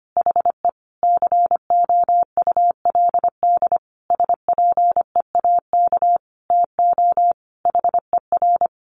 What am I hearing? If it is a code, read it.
Morse code, heard as a keyed tone, about 25 wpm, 700 Hz. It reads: HE COULD SPEAK TO HER
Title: Sentences from Top 1400 Words 25wpm